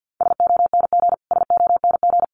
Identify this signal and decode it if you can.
Morse code, heard as a keyed tone, about 50 wpm, 700 Hz. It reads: SONG SONG
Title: Top 500 Words Repeated 50wpm